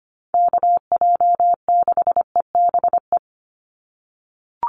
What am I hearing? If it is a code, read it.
Morse code, heard as a keyed tone, about 25 wpm, 690 Hz. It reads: KJ6EBE
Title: USA Call Signs 25wpm